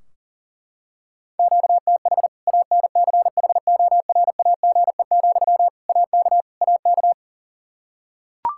Morse code, heard as a keyed tone, about 40 wpm, 700 Hz. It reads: QTH ANCHORAGE, AK AK
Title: Practice QSO - Exchange 1 - Beginning - 40wpm